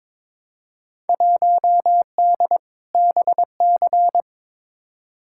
Morse code, heard as a keyed tone, about 22 wpm, 700 Hz. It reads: 1D BC
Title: Field Day Exchanges - 22wpm